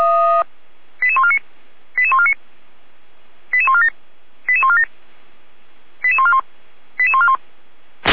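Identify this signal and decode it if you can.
Five-tone selective call (ZVEI), heard as quick runs of five tones, tones of about 70 ms. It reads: 88149 88149 88147 88147 88141 88141
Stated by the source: ZVEI five-tone call received by SDR.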